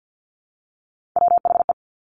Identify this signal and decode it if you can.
Morse code, heard as a keyed tone, about 50 wpm, 700 Hz. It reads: WHE
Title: Top Three Letter Combinations in Words 50wpm